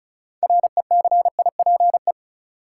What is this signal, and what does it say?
Morse code, heard as a keyed tone, about 34 wpm, 700 Hz. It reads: RECIPE
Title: Sentences from Top 1200 Words - New Word Drill - 35wpm